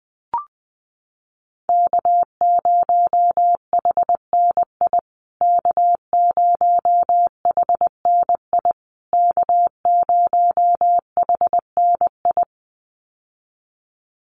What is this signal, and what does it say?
Morse code, heard as a keyed tone, about 20 wpm, 700 Hz. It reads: K0HNI K0HNI K0HNI
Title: USA Call Signs Repeated 3x 20wpm